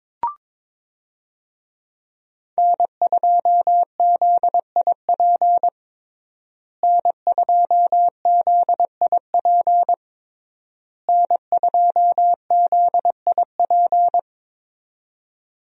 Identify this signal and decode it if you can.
Morse code, heard as a keyed tone, about 22 wpm, 700 Hz. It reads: N2ZIP N2ZIP N2ZIP
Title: USA Call Signs - 5 Characters - Repeated 3x - 22wpm